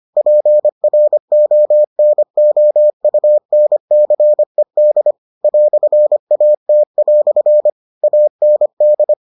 Morse code, heard as a keyed tone, about 25 wpm, 600 Hz. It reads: PRONOUNCED "AT" AND